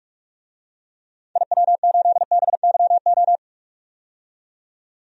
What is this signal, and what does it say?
Morse code, heard as a keyed tone, about 46 wpm, 700 Hz. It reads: IW8BYO
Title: DXCC-100 Call Signs - 6 Characters - 45wpm